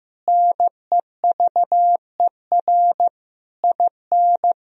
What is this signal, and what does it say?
Morse code, heard as a keyed tone, about 15 wpm, 700 Hz. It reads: NEVER IN
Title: Sentences from Top 1000 Words - Review Part 3 - 15wpm